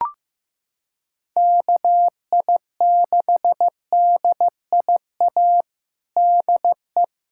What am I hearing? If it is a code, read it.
Morse code, heard as a keyed tone, about 15 wpm, 700 Hz. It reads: KI6DIA DE
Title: USA Callsign DE Callsign 15wpm